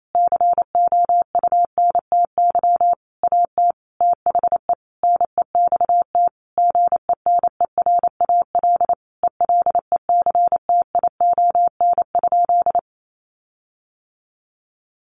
Morse code, heard as a keyed tone, about 28 wpm, 700 Hz. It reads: COUNTY AT THE NEXT GENERAL ELECTION?
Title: Sentences from Top 1500 Words 28wpm